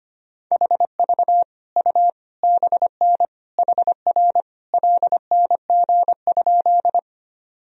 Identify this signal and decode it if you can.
Morse code, heard as a keyed tone, about 25 wpm, 700 Hz. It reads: HV U BN HR LNG?
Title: Sentences from Top 200 Words and Abbreviations 25wpm